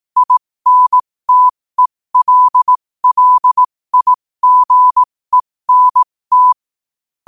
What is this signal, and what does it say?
Morse code, heard as a keyed tone, about 18 wpm, spaced as for 15 wpm, 1000 Hz. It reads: INTELLIGENT
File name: 15wpm